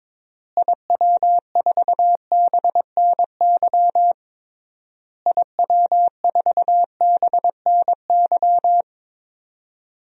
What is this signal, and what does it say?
Morse code, heard as a keyed tone, about 21 wpm, 700 Hz. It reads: IW4BNY IW4BNY
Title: DXCC-100 Call Signs - 6 Characters - Repeated 3x - 22wpm